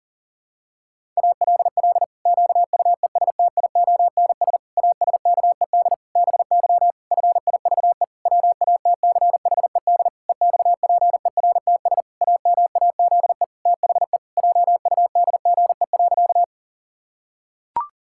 Morse code, heard as a keyed tone, about 40 wpm, 700 Hz. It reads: ALL QUESTIONS ASKED BY FIVE WATCHED EXPERTS AMAZE THE JUDGE.
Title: Pangram Challenge v2 - 40wpm